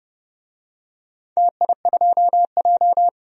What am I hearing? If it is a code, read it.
Morse code, heard as a keyed tone, about 31 wpm, 700 Hz. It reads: TI2J
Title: DXCC-100 Call Signs - 4 Characters - 30wpm